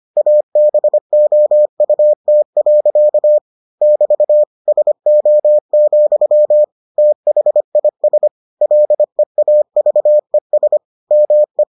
Morse code, heard as a keyed tone, about 25 wpm, 600 Hz. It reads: ABOUT. <BT> SO, THIS LEAVES ME